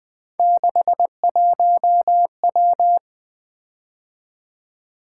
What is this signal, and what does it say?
Morse code, heard as a keyed tone, about 20 wpm, 700 Hz. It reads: 61W
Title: Three Letter-Number Combinations - 20wpm